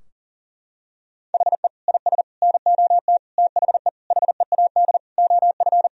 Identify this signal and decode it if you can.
Morse code, heard as a keyed tone, about 40 wpm, 700 Hz. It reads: HE IS NOT THE HEAD OF
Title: Sentences from Top 1000 Words - Review Part 2 - 40wpm